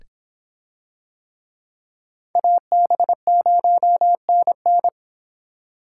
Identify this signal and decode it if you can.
Morse code, heard as a keyed tone, about 24 wpm, 700 Hz. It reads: AB0NN